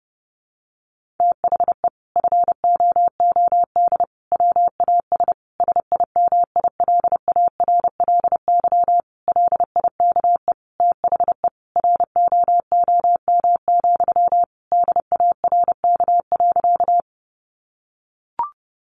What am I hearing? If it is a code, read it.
Morse code, heard as a keyed tone, about 30 wpm, 700 Hz. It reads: THE FOOD WAS SIMILARLY LIKE THE ROOM, DARK.
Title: Sentences from Top 1400 Words 30wpm